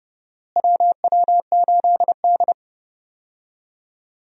Morse code, heard as a keyed tone, about 30 wpm, 700 Hz. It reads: WW8D
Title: USA Call Signs - 4 Characters - Repeated 2x - 30wpm